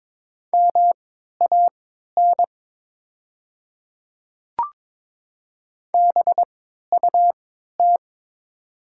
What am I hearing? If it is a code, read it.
Morse code, heard as a keyed tone, about 22 wpm, spaced as for 12 wpm, 690 Hz. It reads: MAN BUT
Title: 3x Character Spacing - Top 300 Words - 22wpm